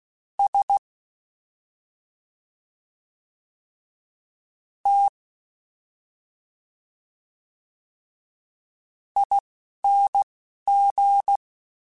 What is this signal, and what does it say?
Morse code, heard as a keyed tone, about 16 wpm, spaced as for 4 wpm, 800 Hz. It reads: STING
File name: Special INGs-mp3s / Special INGs-mp3s / Slow ing @10